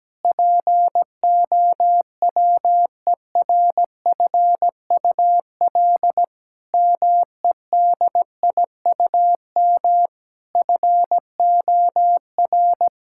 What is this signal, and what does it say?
Morse code, heard as a keyed tone, about 17 wpm, 700 Hz. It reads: POWERFUL MEDIUM FOR
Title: Sentences from Top 1000 Words - Review Part 6 - 17wpm